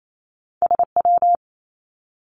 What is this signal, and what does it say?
Morse code, heard as a keyed tone, about 28 wpm, 700 Hz. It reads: SW
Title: Two Challenging Letter-Number Combinations 28wpm